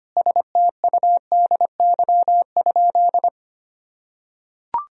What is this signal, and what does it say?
Morse code, heard as a keyed tone, about 25 wpm, 700 Hz. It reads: STUDY?
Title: Sentences from Top 1000 Words - Review Part 1 - 25wpm